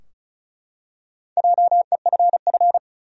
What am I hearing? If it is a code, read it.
Morse code, heard as a keyed tone, about 36 wpm, 700 Hz. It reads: JEFF
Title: USA Common Names - Max 4 Letters - 35wpm